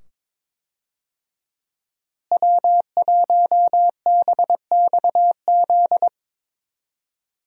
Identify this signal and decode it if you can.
Morse code, heard as a keyed tone, about 22 wpm, 700 Hz. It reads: W1BXZ
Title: USA Call Signs - 5 Characters - Repeated 2x - 22wpm